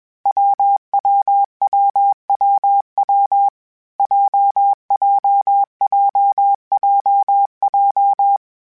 Morse code, heard as a keyed tone, about 21 wpm, 800 Hz. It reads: WWWWW JJJJJ
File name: Difficult Combos / Difficult Combos-mp3s / WPJ WPJ1 @20